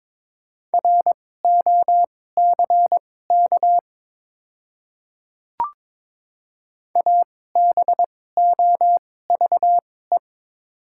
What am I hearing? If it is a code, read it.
Morse code, heard as a keyed tone, about 21 wpm, spaced as for 16 wpm, 700 Hz. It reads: ROCK ABOVE
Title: 2x Character Spacing - Top 500 Words - 22wpm